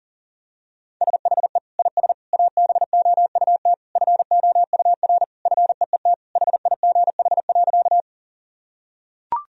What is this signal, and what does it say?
Morse code, heard as a keyed tone, about 37 wpm, 700 Hz. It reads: SHE IS ABOUT FOUR FEET HIGH.
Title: Sentences from Top 1000 Words - Review Part 1 - 40wpm